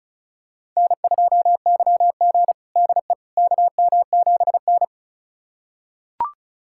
Morse code, heard as a keyed tone, about 36 wpm, 700 Hz. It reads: N2YG DE KM7N
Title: USA Callsign DE Callsign - 4 Characters - 35wpm